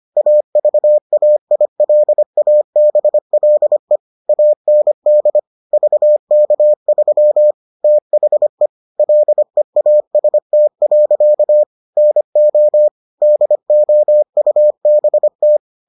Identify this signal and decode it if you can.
Morse code, heard as a keyed tone, about 25 wpm, 600 Hz. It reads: AVAILABLE AND VK3 THE LEAST. NO DOUBT